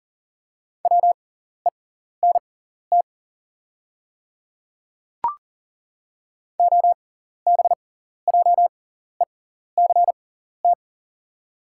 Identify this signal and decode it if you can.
Morse code, heard as a keyed tone, about 40 wpm, spaced as for 14 wpm, 690 Hz. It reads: WENT OBJECT
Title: 6x Character Spacing - Top 500 Words - 40wpm